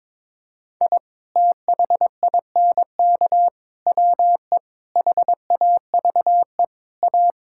Morse code, heard as a keyed tone, about 22 wpm, 700 Hz. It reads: I THINK WE HAVE A
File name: practice / 22wpm / Sentences with 4-Word Phrases - 22wpm